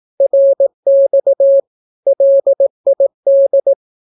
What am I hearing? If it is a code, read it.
Morse code, heard as a keyed tone, about 18 wpm, 550 Hz. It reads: RX LID
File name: naukapodstaw / skroty / skroty